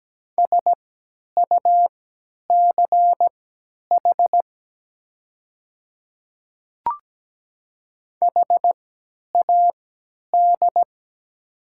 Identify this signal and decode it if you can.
Morse code, heard as a keyed tone, about 17 wpm, spaced as for 10 wpm, 700 Hz. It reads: SUCH HAD
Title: 3x Character Spacing - Top 200 Words - 17wpm